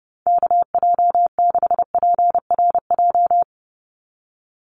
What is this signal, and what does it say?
Morse code, heard as a keyed tone, about 30 wpm, 700 Hz. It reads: KJ6PRJ
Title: USA Call Signs 30wpm